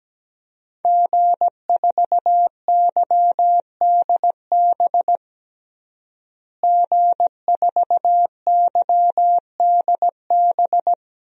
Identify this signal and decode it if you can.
Morse code, heard as a keyed tone, about 17 wpm, 700 Hz. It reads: G4YDB G4YDB
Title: DXCC-100 Call Signs - 5 Characters - Repeated 2x - 17wpm